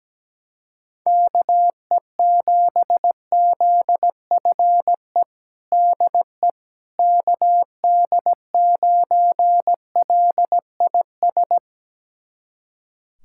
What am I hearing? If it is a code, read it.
Morse code, heard as a keyed tone, about 17 wpm, 700 Hz. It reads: KE7ZFE DE KD9LIS
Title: USA Callsign DE Callsign 17wpm